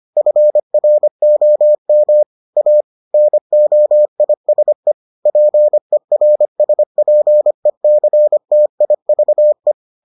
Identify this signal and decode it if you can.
Morse code, heard as a keyed tone, about 25 wpm, 600 Hz. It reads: FROM A NOISE PERSPECTIVE